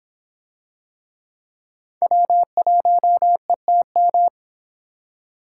Morse code, heard as a keyed tone, about 26 wpm, 700 Hz. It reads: W1ETM